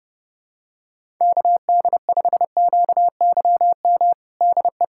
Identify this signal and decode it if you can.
Morse code, heard as a keyed tone, about 30 wpm, 700 Hz. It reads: KD5QYM DE
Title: USA Callsign DE Callsign - 6 Characters - 30wpm